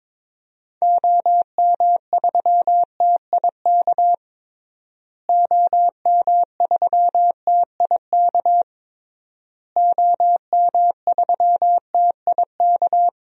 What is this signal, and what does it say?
Morse code, heard as a keyed tone, about 21 wpm, 700 Hz. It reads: OM3TIK OM3TIK OM3TIK
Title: DXCC-100 Call Signs - 6 Characters - Repeated 3x - 22wpm